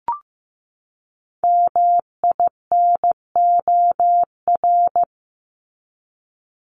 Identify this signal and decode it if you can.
Morse code, heard as a keyed tone, about 15 wpm, 700 Hz. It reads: MINOR
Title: Top 1251-1500 Words 15wpm